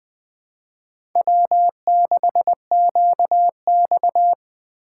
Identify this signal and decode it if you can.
Morse code, heard as a keyed tone, about 20 wpm, 700 Hz. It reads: W6QX
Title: USA Call Signs - 4 Characters - 6x Character Spacing - 20wpm